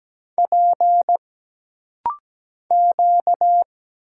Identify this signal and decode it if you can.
Morse code, heard as a keyed tone, about 17 wpm, 700 Hz. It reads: P Q
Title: Single Letter-Number Alt - Rapid-Fire 17wpm